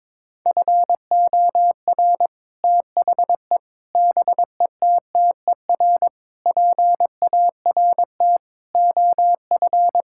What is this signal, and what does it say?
Morse code, heard as a keyed tone, about 22 wpm, 700 Hz. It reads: FOR THE BETTER PART OF
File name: practice / 22wpm / Sets of 6 Words - 22wpm